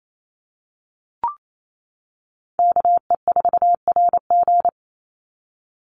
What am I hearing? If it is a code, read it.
Morse code, heard as a keyed tone, about 28 wpm, 700 Hz. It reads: KE4RG
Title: Call Signs - 28wpm